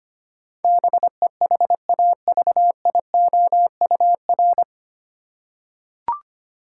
Morse code, heard as a keyed tone, about 25 wpm, 700 Hz. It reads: BEHAVIOUR
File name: Nine Letter Words - 25wpm